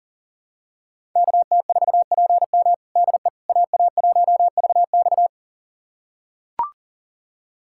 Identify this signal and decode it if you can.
Morse code, heard as a keyed tone, about 40 wpm, 700 Hz. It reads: KT4PM DE AA1VX
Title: USA Callsign DE Callsign - 5 Characters - 40wpm